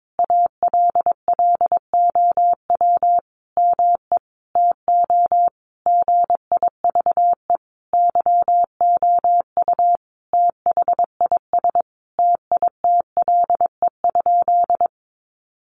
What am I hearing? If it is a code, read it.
Morse code, heard as a keyed tone, about 22 wpm, 700 Hz. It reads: ALLOW ME TO GIVE YOU THIS TITLE?
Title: Sentences from Top 1200 Words 22wpm